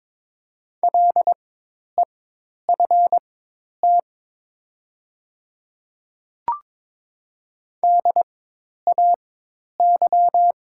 Morse code, heard as a keyed tone, about 22 wpm, spaced as for 10 wpm, 700 Hz. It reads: LEFT DAY
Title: 4x Character Spacing - Top 300 Words - 22wpm